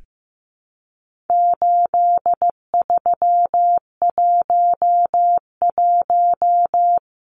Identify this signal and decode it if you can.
Morse code, heard as a keyed tone, about 15 wpm, 700 Hz. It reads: 8311
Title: Numbers - 4-Digit 15wpm